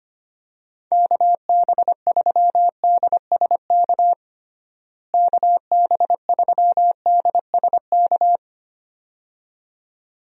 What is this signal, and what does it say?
Morse code, heard as a keyed tone, about 25 wpm, 700 Hz. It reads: KB3DSK KB3DSK
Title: USA Call Signs - 6 Characters - Repeated 2x - 25wpm